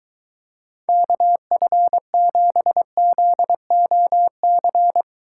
Morse code, heard as a keyed tone, about 23 wpm, 700 Hz. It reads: KF7ZOC